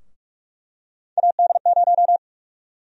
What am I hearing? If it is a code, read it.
Morse code, heard as a keyed tone, about 46 wpm, 700 Hz. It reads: AD0
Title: USA Call Sign Prefixes - 45wpm